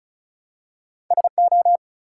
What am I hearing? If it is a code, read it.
Morse code, heard as a keyed tone, about 36 wpm, 700 Hz. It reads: SO
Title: Two Letter Words - 35wpm